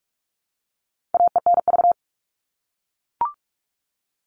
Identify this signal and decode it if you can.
Morse code, heard as a keyed tone, about 44 wpm, 700 Hz. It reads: AENV
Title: All Cut Numbers - 4-Digit 45wpm